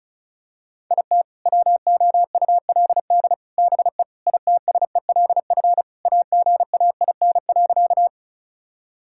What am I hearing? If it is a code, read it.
Morse code, heard as a keyed tone, about 34 wpm, 700 Hz. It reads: IT WOULD BE ITSELF AGAIN.
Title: Sentences from Top 1000 Words - Review Part 5 - 35wpm